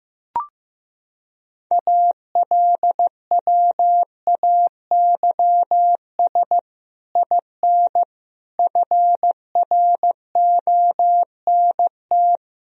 Morse code, heard as a keyed tone, about 16 wpm, 700 Hz. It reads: ALWAYS IN FRONT
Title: Sentences from Top 1000 Words - Review Part 2 - 15wpm